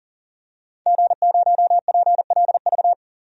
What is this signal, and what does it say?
Morse code, heard as a keyed tone, about 39 wpm, 700 Hz. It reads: G0PLV
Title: DXCC-100 Call Signs - 5 Characters - 40wpm